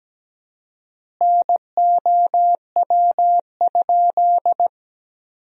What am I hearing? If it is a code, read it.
Morse code, heard as a keyed tone, about 17 wpm, 700 Hz. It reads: NOW?
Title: 4x Word-Spacing - Sentences from Top 200 Words - 17wpm